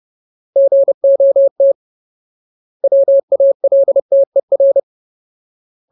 Morse code, heard as a keyed tone, about 30 wpm, 550 Hz. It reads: GOT WALTER
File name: Ultimate Word List / 30WPM Ultimate